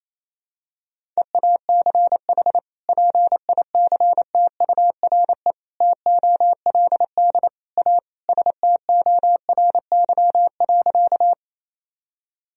Morse code, heard as a keyed tone, about 28 wpm, 700 Hz. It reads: EACH PICTURE TOLD A STORY.
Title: Sentences from Top 1000 Words - Review Part 1 - 28wpm